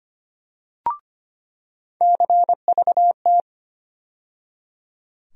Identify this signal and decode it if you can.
Morse code, heard as a keyed tone, about 25 wpm, 700 Hz. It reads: CVT
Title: USA Call Sign Postfixes - 3 Letters 25wpm